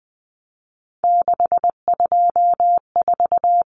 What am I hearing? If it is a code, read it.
Morse code, heard as a keyed tone, about 20 wpm, 700 Hz. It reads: 624
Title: Numbers - 3-Digit 20wpm